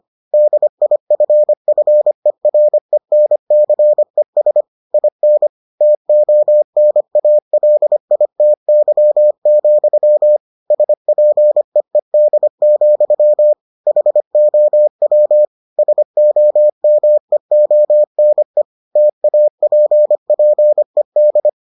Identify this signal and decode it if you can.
Morse code, heard as a keyed tone, about 25 wpm, 600 Hz. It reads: DIFFERENCES IN TONALITY, SPEED, HOW SOMEONE TAPPED